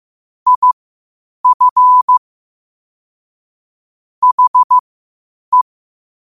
Morse code, heard as a keyed tone, about 15 wpm, spaced as for 8 wpm, 1000 Hz. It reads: IF HE
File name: qotdfiles / 8wpm